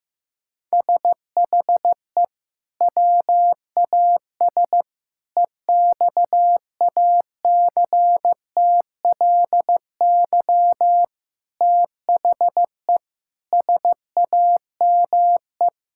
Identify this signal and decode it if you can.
Morse code, heard as a keyed tone, about 15 wpm, 700 Hz. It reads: SHE WAS EXACTLY THE SAME
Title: Sentences from Top 1000 Words - Review Part 6 - 15wpm